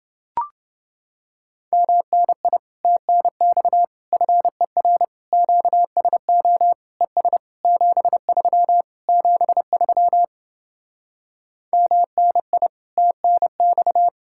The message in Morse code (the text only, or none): MNI TNX FER QSO ES 73 73 MNI TNX